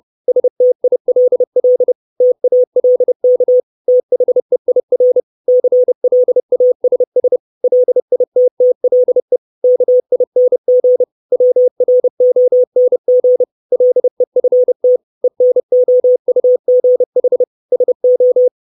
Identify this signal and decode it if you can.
Morse code, heard as a keyed tone, about 30 wpm, 500 Hz. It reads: STILL TALK THEIR CLASS LITTLE KING WRONG LEFT ENOUGH SO
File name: 632 Words 30WPM